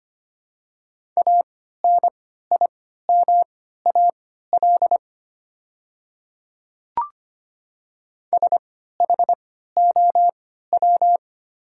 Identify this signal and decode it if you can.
Morse code, heard as a keyed tone, about 25 wpm, spaced as for 14 wpm, 690 Hz. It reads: ANIMAL SHOW
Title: 3x Character Spacing - Top 200 Words - 25wpm